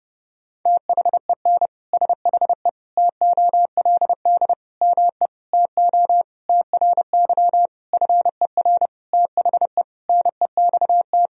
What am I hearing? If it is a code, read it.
Morse code, heard as a keyed tone, about 30 wpm, 700 Hz. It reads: THEN SHE TOLD ME TO TRY FER THE NEXT
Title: Sentences from Top 400 Words and Abbreviations 30wpm